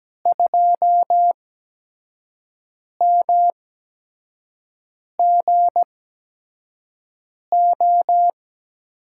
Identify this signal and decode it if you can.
Morse code, heard as a keyed tone, about 17 wpm, spaced as for 5 wpm, 700 Hz. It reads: 2MGO
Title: USA Call Signs - 5 Characters - 8x Character Spacing - 17wpm